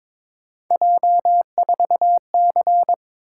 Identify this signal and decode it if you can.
Morse code, heard as a keyed tone, about 26 wpm, 700 Hz. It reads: J4C
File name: Speed-Racing - Three Letter-Number Combinations - 22wpm